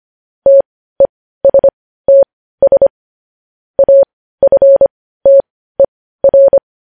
Morse code, heard as a keyed tone, about 25 wpm, spaced as for 15 wpm, 550 Hz. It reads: TESTS AFTER
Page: news-headlines-in-morse-code-at-15-wpm